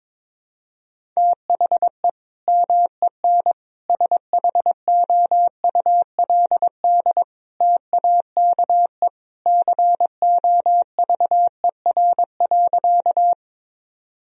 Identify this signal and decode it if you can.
Morse code, heard as a keyed tone, about 21 wpm, 700 Hz. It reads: THE MEN SHOULD TAKE COVER.
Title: Sentences from Top 1000 Words - Review Part 1 - 22wpm